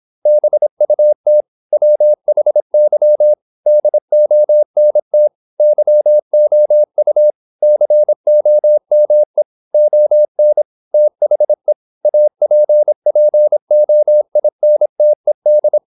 Morse code, heard as a keyed tone, about 26 wpm, 600 Hz. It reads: BUT WHY DONT YOU COME ON THE APPOINTED
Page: Never 26wpm | CW med Gnister